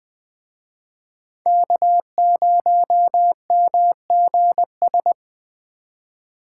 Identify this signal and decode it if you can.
Morse code, heard as a keyed tone, about 21 wpm, 700 Hz. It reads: K0MGS